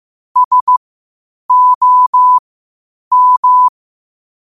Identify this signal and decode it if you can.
Morse code, heard as a keyed tone, about 15 wpm, spaced as for 8 wpm, 1000 Hz. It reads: SOM
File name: qotdfiles / 8wpm